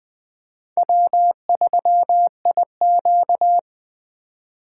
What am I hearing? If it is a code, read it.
Morse code, heard as a keyed tone, about 20 wpm, 700 Hz. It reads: W3IQ
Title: USA Call Signs - 4x Character Spacing - 20wpm